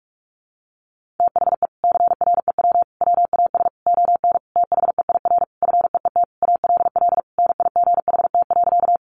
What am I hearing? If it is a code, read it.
Morse code, heard as a keyed tone, about 44 wpm, 700 Hz. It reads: THE CREW WAS ON THEIR FEET ALL NIGHT.
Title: Sentences from Top 1500 Words 45wpm